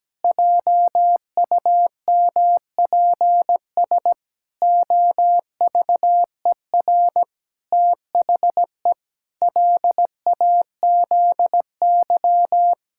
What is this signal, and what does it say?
Morse code, heard as a keyed tone, about 17 wpm, 700 Hz. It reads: JUMPS OVER THE LAZY
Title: Pangram Challenge v2 - 17wpm